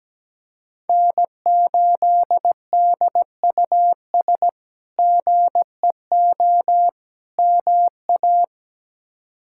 Morse code, heard as a keyed tone, about 17 wpm, 700 Hz. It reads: N8DUS GEO MA
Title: K1USN SST QSO Practice - 17wpm